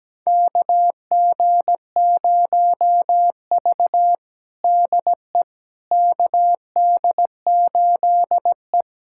Morse code, heard as a keyed tone, about 17 wpm, 700 Hz. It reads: KG0V DE KD8E
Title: USA Callsign DE Callsign - 4 Characters - 17wpm